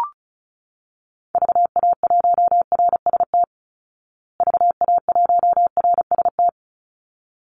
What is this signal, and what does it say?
Morse code, heard as a keyed tone, about 34 wpm, 700 Hz. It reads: VA1RST VA1RST
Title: Call Signs Repeated 3x - 35wpm